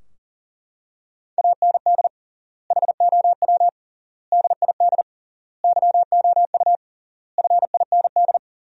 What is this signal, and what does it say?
Morse code, heard as a keyed tone, about 40 wpm, 700 Hz. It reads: AND HOW DID YOU FIND
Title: 3x Word-Spacing - Sentences from Top 200 Words - 40wpm